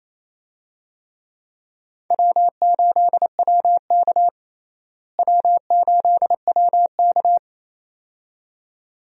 Morse code, heard as a keyed tone, about 27 wpm, 700 Hz. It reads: W8WK W8WK
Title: USA Call Signs - 4 Characters - Repeated 2x - 28wpm